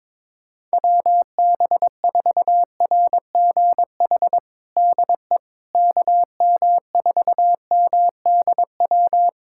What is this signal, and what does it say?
Morse code, heard as a keyed tone, about 21 wpm, 700 Hz. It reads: WB4RGH DE KM4MDW
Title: USA Callsign DE Callsign 22wpm